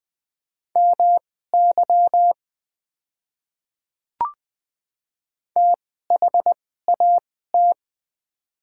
Morse code, heard as a keyed tone, about 20 wpm, spaced as for 14 wpm, 700 Hz. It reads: MY THAT
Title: 2x Character Spacing - Top 200 Words - 20wpm